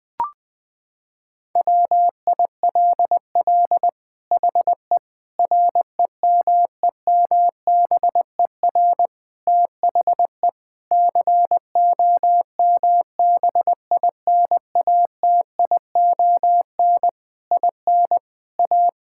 Morse code, heard as a keyed tone, about 21 wpm, 700 Hz. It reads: WILL HE REMEMBER THE COMBINATION IN A
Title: Sentences from Top 1000 Words - Review Part 6 - 20wpm